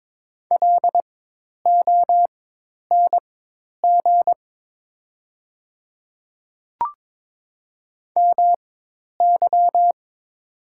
Morse code, heard as a keyed tone, about 22 wpm, spaced as for 10 wpm, 690 Hz. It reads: LONG MY
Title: 4x Character Spacing - Top 200 Words - 22wpm